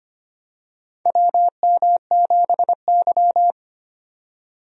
Morse code, heard as a keyed tone, about 25 wpm, 700 Hz. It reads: WM7Y
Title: USA Call Signs - 4 Characters - 25wpm